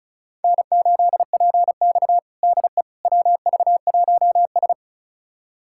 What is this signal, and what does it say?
Morse code, heard as a keyed tone, about 36 wpm, 700 Hz. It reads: N8PX DE WV1S
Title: USA Callsign DE Callsign - 4 Characters - 35wpm